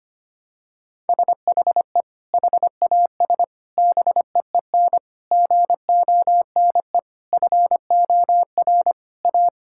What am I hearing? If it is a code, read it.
Morse code, heard as a keyed tone, about 25 wpm, 700 Hz. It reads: SHE HAS BEEN GONE FOR A
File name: practice / 25wpm / Sentences with 3-Word Phrases - 25wpm